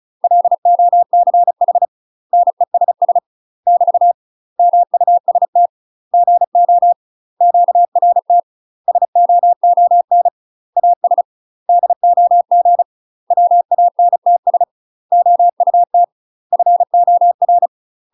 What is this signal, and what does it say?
Morse code, heard as a keyed tone, about 34 wpm, 700 Hz. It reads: LOCH NESS <BT> MUST GO QRT SOON AS DOG WANTS OUT FOR
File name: UK MASTER QSOs TRAINING / 35WPM / UK MASTER QSOs 0030